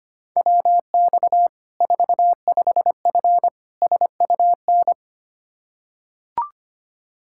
Morse code, heard as a keyed tone, about 25 wpm, 700 Hz. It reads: WX 45F SUN
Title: K1USN SST QSO Practice 25wpm